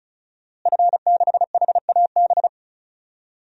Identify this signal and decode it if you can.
Morse code, heard as a keyed tone, about 36 wpm, 700 Hz. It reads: F6HAB
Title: DXCC-100 Call Signs - 5 Characters - Repeated 3x - 35wpm